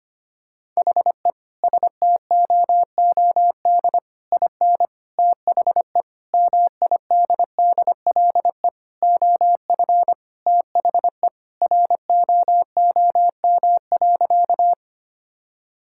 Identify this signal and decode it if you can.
Morse code, heard as a keyed tone, about 25 wpm, 700 Hz. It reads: HE STOOD IN THE MIDDLE OF THE ROOM.
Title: Sentences from Top 1000 Words - Review Part 6 - 25wpm